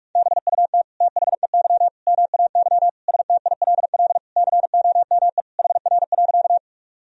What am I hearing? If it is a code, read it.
Morse code, heard as a keyed tone, about 46 wpm, 700 Hz. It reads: BUT THEY MAY STILL COME HR.
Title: Sentences from Top 300 Words and Abbreviations 45wpm